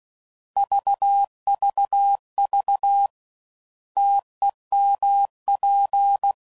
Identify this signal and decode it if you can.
Morse code, heard as a keyed tone, about 16 wpm, 800 Hz. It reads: VVV TEMP